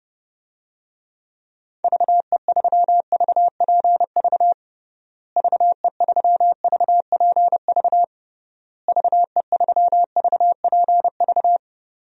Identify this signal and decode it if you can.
Morse code, heard as a keyed tone, about 30 wpm, 700 Hz. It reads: VE3VPV VE3VPV VE3VPV
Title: DXCC-100 Call Signs - 6 Characters - Repeated 3x - 30wpm